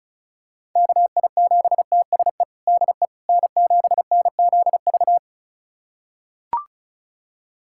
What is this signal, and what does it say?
Morse code, heard as a keyed tone, about 34 wpm, 700 Hz. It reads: KI7TSE DE N7NZV
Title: USA Callsign DE Callsign 35wpm